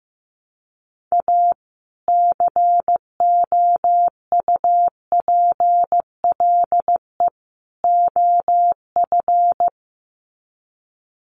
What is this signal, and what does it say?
Morse code, heard as a keyed tone, about 15 wpm, 700 Hz. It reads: A COUPLE OF
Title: Sets of 3 Words 15wpm